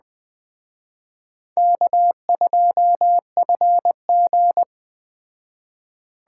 Morse code, heard as a keyed tone, about 20 wpm, 700 Hz. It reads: K2FG
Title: Speed Racing - USA 1x2 Call Signs 17wpm